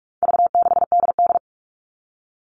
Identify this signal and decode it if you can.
Morse code, heard as a keyed tone, about 44 wpm, 700 Hz. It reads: V6DD
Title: All Cut Numbers - 4-Digit 45wpm